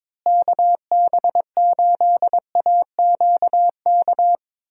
Morse code, heard as a keyed tone, about 22 wpm, 700 Hz. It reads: KB8AQK
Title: USA Call Signs - 6 Characters - Repeated 3x - 22wpm